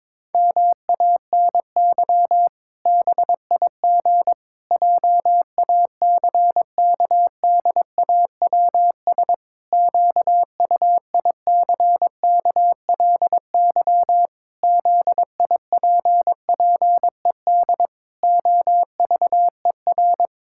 Morse code, heard as a keyed tone, about 22 wpm, 700 Hz. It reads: MANY BIG JACKDAWS QUICKLY ZIPPED OVER
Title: Pangram Challenge v2 - 22wpm